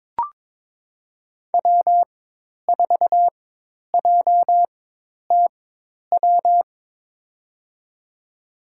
Morse code, heard as a keyed tone, about 22 wpm, spaced as for 10 wpm, 700 Hz. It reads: W4JTW
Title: USA Call Signs - 5 Characters - 4x Character Spacing - 22wpm